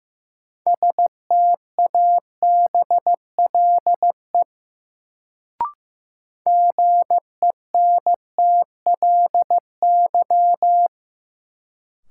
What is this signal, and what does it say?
Morse code, heard as a keyed tone, about 15 wpm, 700 Hz. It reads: STABLE GENTLY
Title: Sentences from Top 1500 Words - New Word Drill Alt - 15wpm